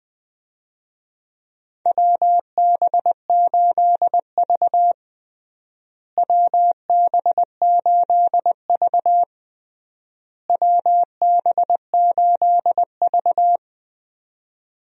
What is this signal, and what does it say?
Morse code, heard as a keyed tone, about 20 wpm, 700 Hz. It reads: WB8V WB8V WB8V
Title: USA Call Signs - 4 Characters - Repeated 3x - 20wpm